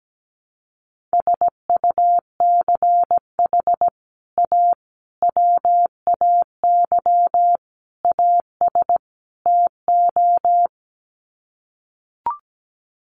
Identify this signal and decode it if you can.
Morse code, heard as a keyed tone, about 17 wpm, 700 Hz. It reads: SUCH A WAY AS TO
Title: Sets of 5 Words Encore2 17wpm